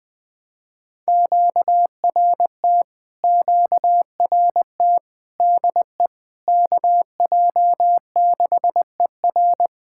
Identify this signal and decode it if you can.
Morse code, heard as a keyed tone, about 20 wpm, 690 Hz. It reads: QRT QRT DE KJ6ER
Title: POTA QSO Practice - 20wpm